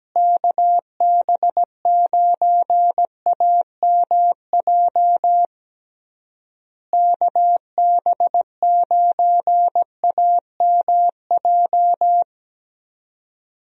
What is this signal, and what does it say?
Morse code, heard as a keyed tone, about 17 wpm, 700 Hz. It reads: KB9AMJ KB9AMJ
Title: USA Call Signs - 6 Characters - Repeated 3x - 17wpm